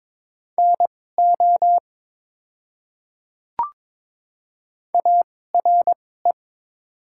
Morse code, heard as a keyed tone, about 22 wpm, spaced as for 16 wpm, 700 Hz. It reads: NO ARE
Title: 2x Character Spacing - Top 300 Words - 22wpm